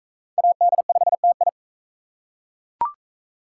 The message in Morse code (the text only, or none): AD5TI